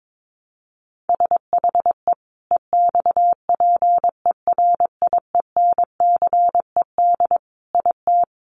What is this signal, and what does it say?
Morse code, heard as a keyed tone, about 22 wpm, 700 Hz. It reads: SHE EXPERIENCED IT
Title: Sentences from Top 1100 Words 22wpm